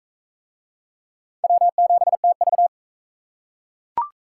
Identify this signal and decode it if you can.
Morse code, heard as a keyed tone, about 38 wpm, 700 Hz. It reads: W7TV